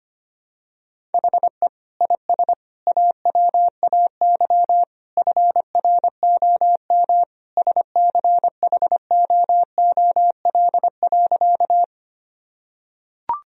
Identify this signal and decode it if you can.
Morse code, heard as a keyed tone, about 25 wpm, 700 Hz. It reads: HE IS AWAY FROM SCHOOL.
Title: Sentences from Top 1000 Words - Review Part 1 - 25wpm